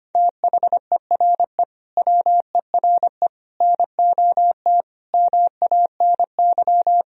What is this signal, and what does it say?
Morse code, heard as a keyed tone, about 25 wpm, 700 Hz. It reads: THERE WERE NOT MANY
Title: Sentences from Top 1000 Words - Review Part 1 - 25wpm